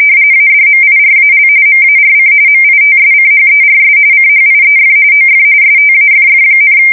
RTTY, about 45.45 baud. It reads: ADMIRAL MARC MITSCHERS CHIEF OF STAFF,
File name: fsk170 / USS-Massachusetts